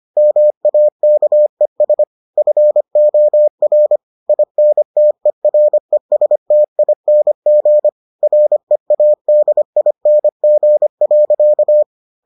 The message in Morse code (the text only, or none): MAKES FOR INTERESTING READING.